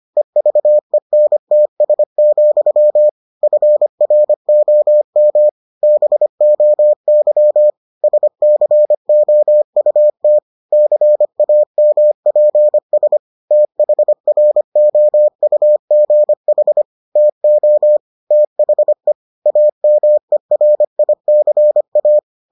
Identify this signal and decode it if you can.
Morse code, heard as a keyed tone, about 25 wpm, 600 Hz. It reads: EVENTS, FROM BOY SCOUT CAMPS THROUGH TO THE AMERICA